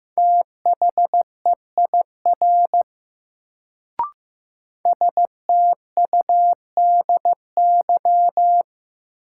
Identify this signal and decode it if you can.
Morse code, heard as a keyed tone, about 15 wpm, 700 Hz. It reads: THEIR STUDY
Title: Sentences from Top 300 Words - New Word Drill - 15wpm